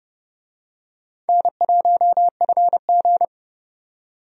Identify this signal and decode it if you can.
Morse code, heard as a keyed tone, about 31 wpm, 700 Hz. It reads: N1FG
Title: USA Call Signs - 4 Characters - 30wpm